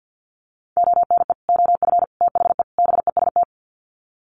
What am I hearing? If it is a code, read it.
Morse code, heard as a keyed tone, about 48 wpm, 700 Hz. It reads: ONE OF THE BEST
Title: Sets of 4 Words 50wpm